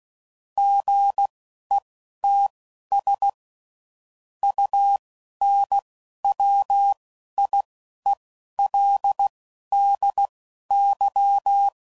Morse code, heard as a keyed tone, about 16 wpm, spaced as for 11 wpm, 800 Hz. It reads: GETS UNWIELDY
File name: CW Code Groups @10WPM / Technical Subjects / AND Gate